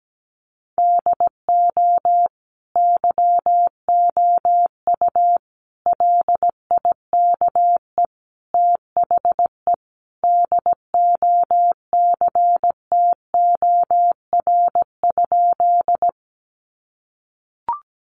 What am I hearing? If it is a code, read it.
Morse code, heard as a keyed tone, about 17 wpm, 700 Hz. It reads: DO YOU LIKE THE DOCTOR?
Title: Sentences from Top 1500 Words 17wpm